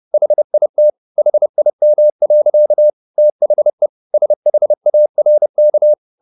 Morse code, heard as a keyed tone, about 30 wpm, 600 Hz. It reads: HIT HIM. THE SHARK